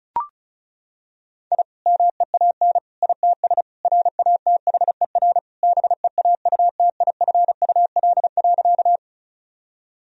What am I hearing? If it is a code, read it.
Morse code, heard as a keyed tone, about 36 wpm, 700 Hz. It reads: I MEAN ITS RATHER BEAUTIFUL.
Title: Sentences from Top 1000 Words - Review Part 4 - 35wpm